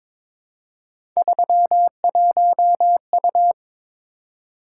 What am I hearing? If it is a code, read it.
Morse code, heard as a keyed tone, about 23 wpm, 700 Hz. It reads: 31U
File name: practice / 22wpm / Three Letter-Number Combinations - 22wpm